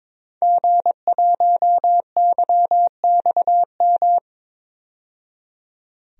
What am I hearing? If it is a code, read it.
Morse code, heard as a keyed tone, about 22 wpm, 700 Hz. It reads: G1YXM
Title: DXCC-100 Call Signs - 5 Characters - 22wpm